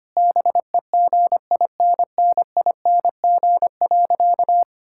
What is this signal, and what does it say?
Morse code, heard as a keyed tone, about 25 wpm, 700 Hz. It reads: BEGINNING.
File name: practice / 25wpm / Sentences with 4-Word Phrases - 25wpm